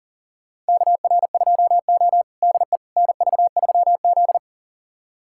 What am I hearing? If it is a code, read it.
Morse code, heard as a keyed tone, about 40 wpm, 700 Hz. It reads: KR2O DE NV3Z
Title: USA Callsign DE Callsign - 4 Characters - 40wpm